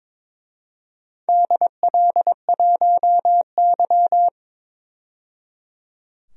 Morse code, heard as a keyed tone, about 22 wpm, 700 Hz. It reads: DL1Y
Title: DXCC-100 Call Signs - 4 Characters - 22wpm